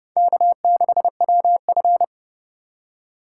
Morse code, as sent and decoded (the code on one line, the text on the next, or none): -.- -.... .-- ..-.
K6WF